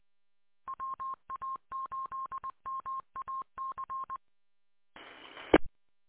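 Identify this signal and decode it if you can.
Morse code, heard as a keyed tone, about 20 wpm, 1070 Hz. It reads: WA8MAC E
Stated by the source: Morse recorded from WA8MAC 147.200 repeater audio archive.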